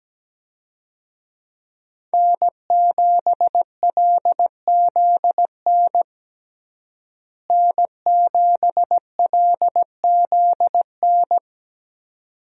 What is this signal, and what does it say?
Morse code, heard as a keyed tone, about 17 wpm, 700 Hz. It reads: N7LZN N7LZN
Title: USA Call Signs - 5 Characters - Repeated 2x - 17wpm